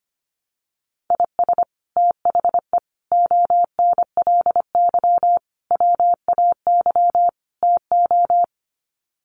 Morse code, heard as a keyed tone, about 25 wpm, 700 Hz. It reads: IS THE ONLY WAY TO
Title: Sets of 5 Words Encore 25wpm